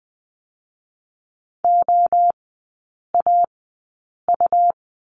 Morse code, heard as a keyed tone, about 20 wpm, spaced as for 8 wpm, 690 Hz. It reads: OAU
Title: 145 Lesson 24e - Letter y - Three Characters 20wpm